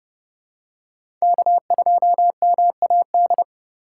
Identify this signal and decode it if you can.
Morse code, heard as a keyed tone, about 32 wpm, 700 Hz. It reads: K2MAD
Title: USA Call Signs - 5 Characters - 2x Character Spacing - 30wpm